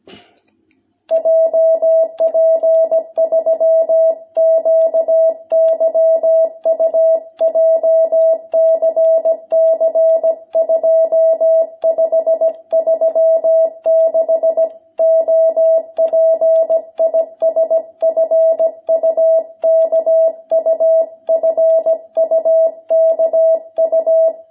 Morse code, heard as a keyed tone, about 17 wpm, 660 Hz. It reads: JP3QYU JCC 2536 OP IS FUKU FUKU
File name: jh8cbh / 2021cw